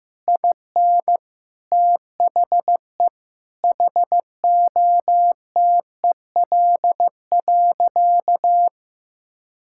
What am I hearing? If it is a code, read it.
Morse code, heard as a keyed tone, about 15 wpm, 700 Hz. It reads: IN THE HOTEL.
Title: Sentences from Top 1600 Words 15wpm